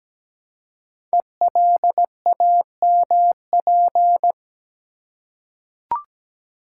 Morse code, heard as a keyed tone, about 17 wpm, 700 Hz. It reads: ELAMP
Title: Code Groups - 5 Letters - 17wpm